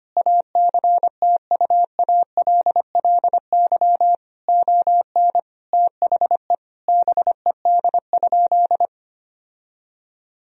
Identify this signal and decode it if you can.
Morse code, heard as a keyed tone, about 25 wpm, 700 Hz. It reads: ACTUALLY ON THE BED?
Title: Sentences from Top 1000 Words - Review Part 4 - 25wpm